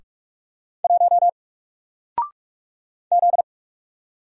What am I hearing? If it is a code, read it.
Morse code, heard as a keyed tone, about 45 wpm, 700 Hz. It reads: 1 Z
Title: Single Character (Letter:Number:Punctuation:Prosign) Alt 45wpm